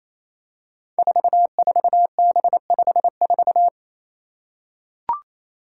Morse code, heard as a keyed tone, about 28 wpm, 700 Hz. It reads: 44B54
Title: Five Challenging Letter-Number Combinations 28wpm